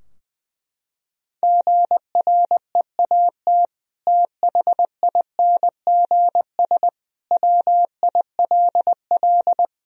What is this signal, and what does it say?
Morse code, heard as a keyed tone, about 21 wpm, 700 Hz. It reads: GREAT THINGS WILL
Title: Sentences from Top 1000 Words - Review Part 4 - 20wpm